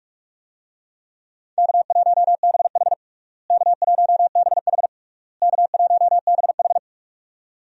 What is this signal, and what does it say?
Morse code, heard as a keyed tone, about 46 wpm, 700 Hz. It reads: K1BH K1BH K1BH
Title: USA Call Signs - 4 Characters - Repeated 3x - 45wpm